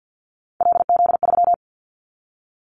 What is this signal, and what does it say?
Morse code, heard as a keyed tone, about 50 wpm, 700 Hz. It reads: LZ3
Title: Callsign Prefixes for Top 50 DXCC - 50wpm